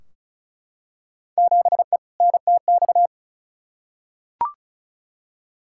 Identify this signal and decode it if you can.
Morse code, heard as a keyed tone, about 35 wpm, 700 Hz. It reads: 7E NTX
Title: Field Day Exchanges Encore - 35wpm